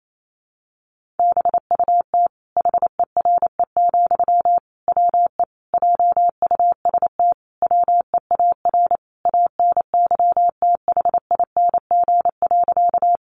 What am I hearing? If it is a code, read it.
Morse code, heard as a keyed tone, about 28 wpm, 700 Hz. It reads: BUT HERE, WE JUST WEAR ANYTHING.
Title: Sentences from Top 1500 Words 28wpm